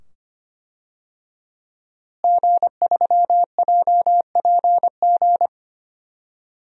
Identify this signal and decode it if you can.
Morse code, heard as a keyed tone, about 25 wpm, 700 Hz. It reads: G3JPG
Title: DXCC-100 Call Signs - 5 Characters - Repeated 2x - 25wpm